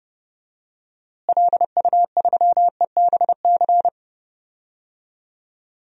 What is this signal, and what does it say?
Morse code, heard as a keyed tone, about 30 wpm, 700 Hz. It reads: LU3EBC
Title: DXCC-100 Call Signs - 6 Characters - Repeated 3x - 30wpm